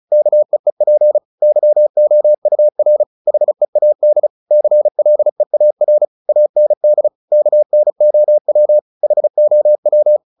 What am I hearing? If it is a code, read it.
Morse code, heard as a keyed tone, about 34 wpm, 600 Hz. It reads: KEEP YOUR HEAD CLEAR AND KNOW HOW
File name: the old man and the sea 2 35wpm